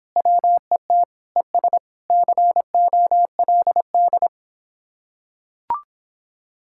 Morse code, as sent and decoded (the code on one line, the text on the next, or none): .-- . - / . ... / -.-. --- .-.. -..
WET ES COLD